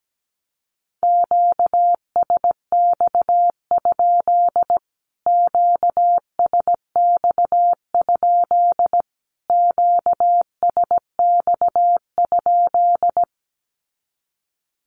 Morse code, heard as a keyed tone, about 17 wpm, 700 Hz. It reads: QSX? QSX? QSX?
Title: Q-Codes Repeated 3x 17wpm